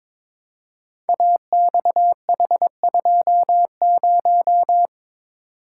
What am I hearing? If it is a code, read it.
Morse code, heard as a keyed tone, about 22 wpm, 700 Hz. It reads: AXH20
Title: Code Groups - 5 Letter:Numbers - 22wpm